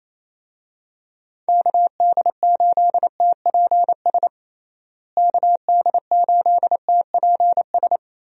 Morse code, heard as a keyed tone, about 28 wpm, 700 Hz. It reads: KD8TPS KD8TPS
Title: USA Call Signs - 6 Characters - Repeated 2x - 28wpm